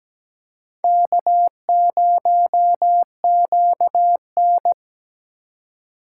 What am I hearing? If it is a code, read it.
Morse code, heard as a keyed tone, about 17 wpm, 700 Hz. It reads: K0QN
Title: USA Call Signs - 2x Character Spacing - 17wpm